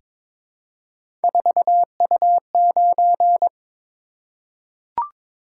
Morse code, heard as a keyed tone, about 23 wpm, 700 Hz. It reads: 4U9
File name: practice / 22wpm / Three Letter-Number Combinations - 22wpm